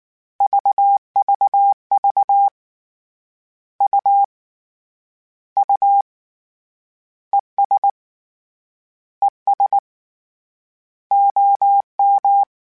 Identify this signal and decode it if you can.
Morse code, heard as a keyed tone, about 19 wpm, 800 Hz. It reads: VVV U U ES ES OM
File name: Abreviations / Abreviations mp3s / Abreviations@18